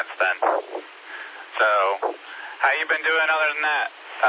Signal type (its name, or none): Emergency Alert System (SAME) tones